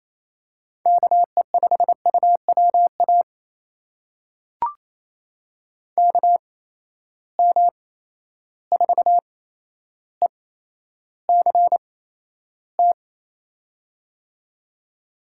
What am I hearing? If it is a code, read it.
Morse code, heard as a keyed tone, about 28 wpm, spaced as for 12 wpm, 700 Hz. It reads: KE5UWA KM4ECT
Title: USA Call Signs - 6 Characters - 8x Character Spacing - 28wpm